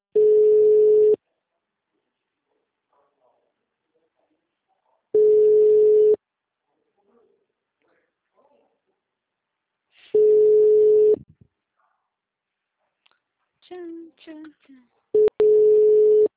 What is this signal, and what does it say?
Ringback tone (Europe, 425 Hz), with speech.